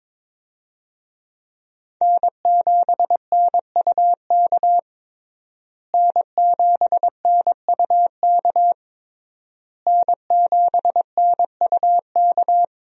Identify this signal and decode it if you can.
Morse code, heard as a keyed tone, about 22 wpm, 700 Hz. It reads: N7NUK N7NUK N7NUK
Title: USA Call Signs - 5 Characters - Repeated 3x - 22wpm